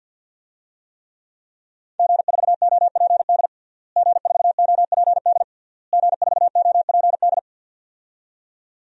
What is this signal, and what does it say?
Morse code, heard as a keyed tone, about 52 wpm, 700 Hz. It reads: G4OPD G4OPD G4OPD
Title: DXCC-100 Call Signs - 5 Characters - Repeated 3x - 50wpm